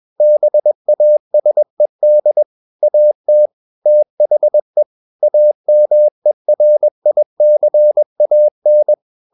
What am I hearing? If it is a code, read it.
Morse code, heard as a keyed tone, about 21 wpm, 600 Hz. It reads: BASED AT THE AMERICAN
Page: CW med Gnister